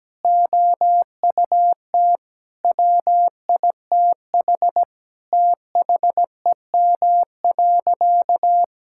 Morse code, heard as a keyed tone, about 17 wpm, 700 Hz. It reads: OUT WITH THEM.
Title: Sentences from Top 1600 Words 17wpm